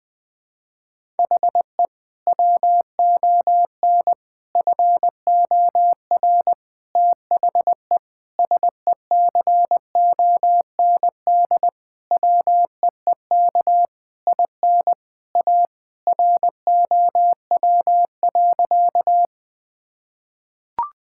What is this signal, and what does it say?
Morse code, heard as a keyed tone, about 21 wpm, 700 Hz. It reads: HE WON FOR THE SECOND WEEK IN A ROW.
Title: Sentences with 7-Word Phrases - 20wpm